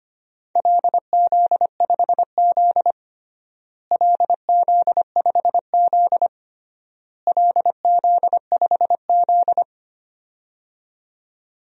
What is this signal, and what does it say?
Morse code, heard as a keyed tone, about 25 wpm, 700 Hz. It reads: LZ5Z LZ5Z LZ5Z
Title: DXCC-100 Call Signs - 4 Characters - Repeated 3x - 25wpm